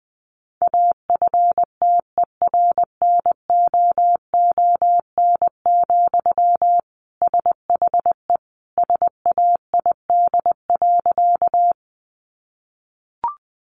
Morse code, heard as a keyed tone, about 20 wpm, 700 Hz. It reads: AFTERNOON, SHE SAID.
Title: Sentences from Top 1500 Words 20wpm